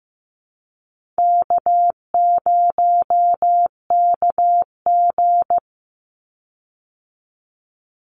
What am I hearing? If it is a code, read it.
Morse code, heard as a keyed tone, about 14 wpm, 700 Hz. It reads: K0KG
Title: Easy Call Signs Repeated 2x - 15wpm